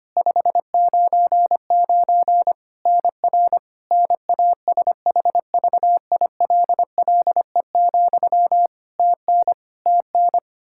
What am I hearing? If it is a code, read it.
Morse code, heard as a keyed tone, about 25 wpm, 700 Hz. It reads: 599 NR NASHVILLE, TN TN
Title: Practice QSO - Exchange 1 - Beginning - 25wpm